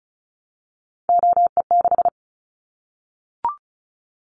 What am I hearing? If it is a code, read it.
Morse code, heard as a keyed tone, about 34 wpm, 700 Hz. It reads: OE6
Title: Callsign Prefixes for Top 100 DXCC - 35wpm